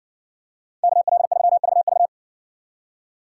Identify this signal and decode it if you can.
Morse code, heard as a keyed tone, about 60 wpm, 700 Hz. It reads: KL3VV